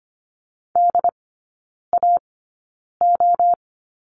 Morse code, heard as a keyed tone, about 25 wpm, spaced as for 9 wpm, 700 Hz. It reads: DAO
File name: course / 25:10wpm / 093 Lesson 16e - Letter c - Three Characters 25wpm